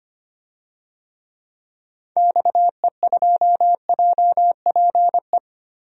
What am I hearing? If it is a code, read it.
Morse code, heard as a keyed tone, about 25 wpm, 700 Hz. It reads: XE2JPE
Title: DXCC-100 Call Signs - 6 Characters - Repeated 2x - 25wpm